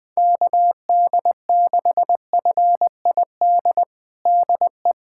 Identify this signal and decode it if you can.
Morse code, heard as a keyed tone, about 20 wpm, 700 Hz. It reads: KD6FID DE
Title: USA Callsign DE Callsign - 6 Characters - 20wpm